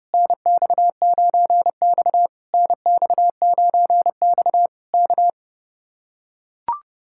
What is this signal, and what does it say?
Morse code, heard as a keyed tone, about 31 wpm, 700 Hz. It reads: NX9X NX9X K